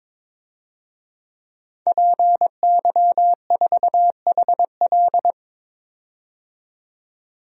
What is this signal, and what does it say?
Morse code, heard as a keyed tone, about 21 wpm, 700 Hz. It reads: PY4HL
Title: DXCC-100 Call Signs - 5 Characters - 22wpm